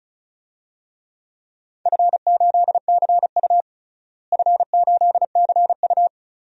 Morse code, heard as a keyed tone, about 36 wpm, 700 Hz. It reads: F8CU F8CU
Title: DXCC-100 Call Signs - 4 Characters - Repeated 2x - 35wpm